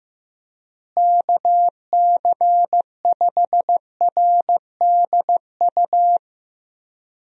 Morse code, heard as a keyed tone, about 15 wpm, 700 Hz. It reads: KC5RDU
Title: USA Call Signs - 6 Characters - Repeated 3x - 15wpm